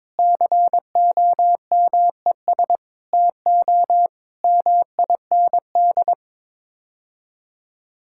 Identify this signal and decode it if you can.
Morse code, heard as a keyed tone, about 22 wpm, 700 Hz. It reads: COMES TO MIND
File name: practice / 22wpm / Sets of 7 Words - 22wpm